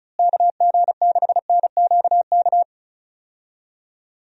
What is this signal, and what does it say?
Morse code, heard as a keyed tone, about 36 wpm, 700 Hz. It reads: KG6NQK
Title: USA Call Signs - 6 Characters - Repeated 2x - 35wpm